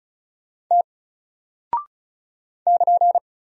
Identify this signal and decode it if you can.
Morse code, heard as a keyed tone, about 36 wpm, 700 Hz. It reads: T <KN>
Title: Single Character (Letter:Number:Punctuation:Prosign) 35wpm